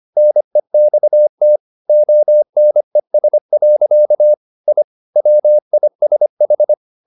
Morse code, heard as a keyed tone, about 25 wpm, 600 Hz. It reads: NEXT ONES. I WISH